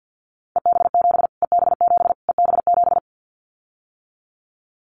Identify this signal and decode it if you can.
Morse code, heard as a keyed tone, about 50 wpm, 700 Hz. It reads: EB7 EB7 EB7
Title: Callsign Prefixes for Top 25 DXCC - Repeated 3x - 50wpm